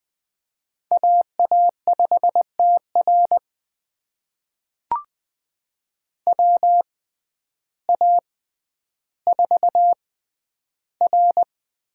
Morse code, heard as a keyed tone, about 20 wpm, spaced as for 11 wpm, 700 Hz. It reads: AA5TR WA4R
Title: USA Call Signs - 6x Character Spacing - 20wpm